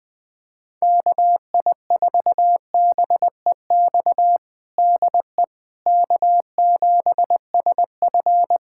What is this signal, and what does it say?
Morse code, heard as a keyed tone, about 20 wpm, 700 Hz. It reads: KI4BEX DE K7SF
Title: Practice QSO - Exchange 4 - Ending - 20wpm